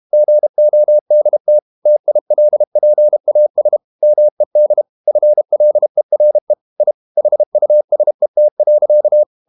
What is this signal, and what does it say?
Morse code, heard as a keyed tone, about 32 wpm, 600 Hz. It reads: GODT TILPAS MED FLERE I HUSET.
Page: Ferie 32 wpm